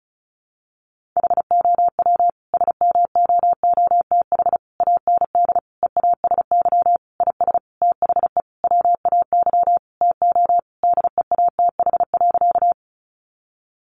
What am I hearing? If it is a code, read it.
Morse code, heard as a keyed tone, about 34 wpm, 700 Hz. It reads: HOW SMOOTH AND EASY IS THE WAY TO DEATH.
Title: Sentences from Top 1400 Words 35wpm